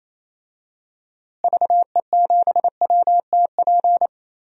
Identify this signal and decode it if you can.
Morse code, heard as a keyed tone, about 28 wpm, 700 Hz. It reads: VE7WTP
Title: DXCC-100 Call Signs - 6 Characters - Repeated 3x - 28wpm